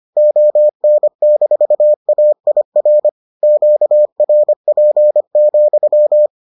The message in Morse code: ON-AIR QRP,

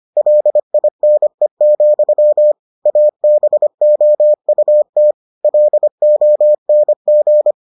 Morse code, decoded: LINE, ABOUT LONG

